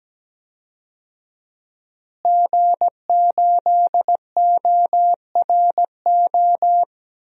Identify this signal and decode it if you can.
Morse code, heard as a keyed tone, about 17 wpm, 700 Hz. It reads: G8ORO
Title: DXCC-100 Call Signs - 5 Characters - Repeated 3x - 17wpm